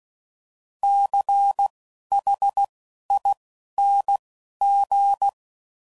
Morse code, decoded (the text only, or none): CHING